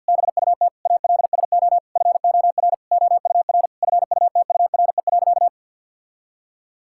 Morse code, heard as a keyed tone, about 50 wpm, 700 Hz. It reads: BUT ALSO FOR OUR FUTURE.
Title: Sentences from Top 1000 Words - Review Part 6 - 50wpm